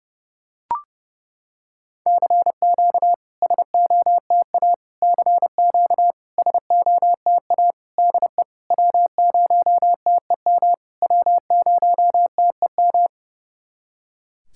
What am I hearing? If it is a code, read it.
Morse code, heard as a keyed tone, about 30 wpm, 700 Hz. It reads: CQ SOTA CQ SOTA DE W0TEM W0TEM
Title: SOTA QSO Practice 30wpm